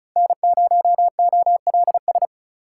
Morse code, heard as a keyed tone, about 36 wpm, 700 Hz. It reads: N0OLS
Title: USA Call Signs - 5 Characters - 4x Character Spacing - 35wpm